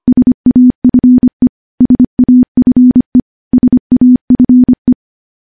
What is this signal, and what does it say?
Morse code, heard as a keyed tone, about 25 wpm, 250 Hz. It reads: SAFE SAFE SAFE
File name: morse / morse-training-25WPM